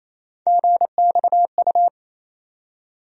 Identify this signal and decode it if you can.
Morse code, heard as a keyed tone, about 28 wpm, 700 Hz. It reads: GXU
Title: USA Call Sign Postfixes - 3 Letters 28wpm